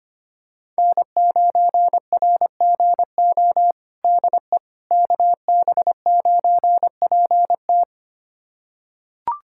Morse code, heard as a keyed tone, about 25 wpm, 700 Hz. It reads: N9RGO DE KB9PT
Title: USA Callsign DE Callsign - 5 Characters - 25wpm